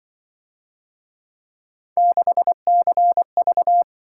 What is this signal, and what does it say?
Morse code, heard as a keyed tone, about 24 wpm, 700 Hz. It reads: 6CV
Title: Speed-Racing - Three Letter-Number Combinations - 20wpm